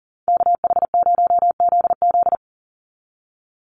Morse code, heard as a keyed tone, about 40 wpm, 700 Hz. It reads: KH0ZZ
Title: Call Signs Repeated 2x - 40wpm